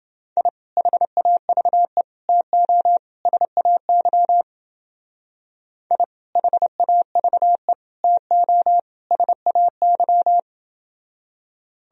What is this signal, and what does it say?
Morse code, heard as a keyed tone, about 28 wpm, 700 Hz. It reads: I HAVE TO SAY I HAVE TO SAY